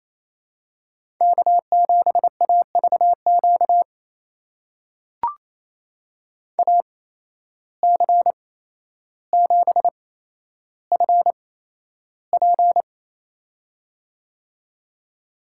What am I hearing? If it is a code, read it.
Morse code, heard as a keyed tone, about 28 wpm, spaced as for 12 wpm, 700 Hz. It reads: K7AVQ AC7FP
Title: USA Call Signs - 5 Characters - 8x Character Spacing - 28wpm